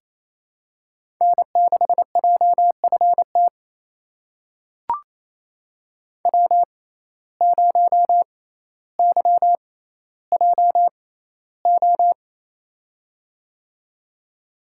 Morse code, heard as a keyed tone, about 28 wpm, spaced as for 14 wpm, 700 Hz. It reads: N6JFT W0YJO
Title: USA Call Signs - 5 Characters - 6x Character Spacing - 28wpm